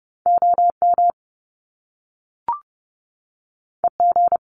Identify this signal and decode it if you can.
Morse code, heard as a keyed tone, about 30 wpm, 700 Hz. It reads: OM EG
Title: Top Two Letter Combinations in Words 30wpm